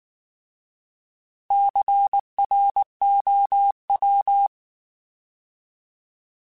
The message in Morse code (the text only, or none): CROW